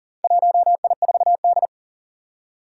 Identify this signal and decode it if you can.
Morse code, heard as a keyed tone, about 40 wpm, 700 Hz. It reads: 1I4D
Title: Four Challenging Letter-Number Combinations 40wpm